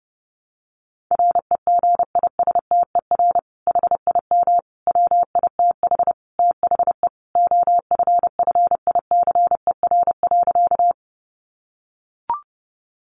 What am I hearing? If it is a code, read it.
Morse code, heard as a keyed tone, about 30 wpm, 700 Hz. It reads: REGISTER HIM WITH THE OFFICER.
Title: Sentences from Top 1200 Words 30wpm